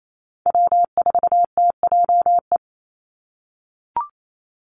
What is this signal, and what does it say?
Morse code, heard as a keyed tone, about 28 wpm, 700 Hz. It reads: W4TJE
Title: Call Signs - 28wpm